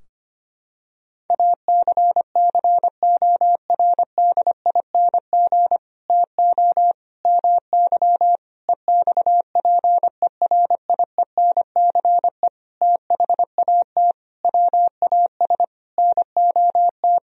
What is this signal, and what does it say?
Morse code, heard as a keyed tone, about 25 wpm, 700 Hz. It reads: ACCORDING TO MY EXPERIENCE THAT WAS NOT
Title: Sentences from Top 1000 Words - Review Part 4 - 25wpm